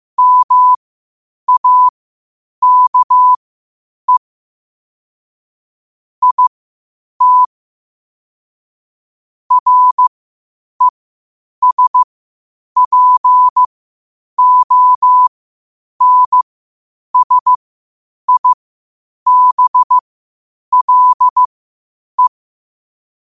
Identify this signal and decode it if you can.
Morse code, heard as a keyed tone, about 15 wpm, spaced as for 8 wpm, 1000 Hz. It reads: MAKE IT RESPONSIBLE
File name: qotdfiles / 8wpm